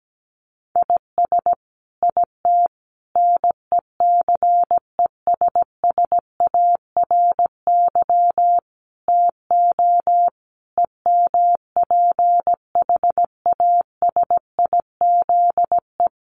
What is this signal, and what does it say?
Morse code, heard as a keyed tone, about 17 wpm, 700 Hz. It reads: IS IT NECESSARY TO EMPHASIZE
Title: Sentences from Top 1500 Words 17wpm